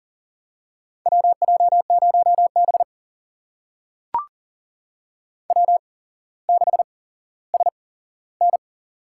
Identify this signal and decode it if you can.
Morse code, heard as a keyed tone, about 40 wpm, spaced as for 17 wpm, 700 Hz. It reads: WJ0B W6SN
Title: USA Call Signs - 4 Characters - 8x Character Spacing - 40wpm